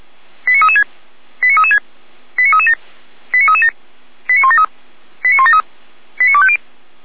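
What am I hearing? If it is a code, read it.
Five-tone selective call (ZVEI), heard as quick runs of five tones, tones of about 70 ms. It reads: 89336 89336 89337 89337 89173 89173 89250
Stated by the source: ZVEI five-tone selective call received by SDR, from the Feuerwehr Orlamünde fire brigade server.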